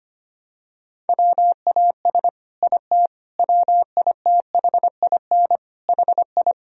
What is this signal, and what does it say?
Morse code, heard as a keyed tone, about 25 wpm, 700 Hz. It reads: WAS IT WITHIN HI
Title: Sentences from Top 1000 Words - Review Part 4 - 25wpm